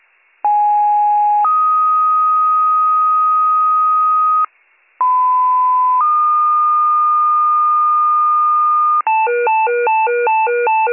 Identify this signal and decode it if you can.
Two-tone page, heard as a one-second tone, then a three-second tone, about 830 then 1250 Hz, about 1010 then 1220 Hz. It is heard from 0.4 to 9.0 s.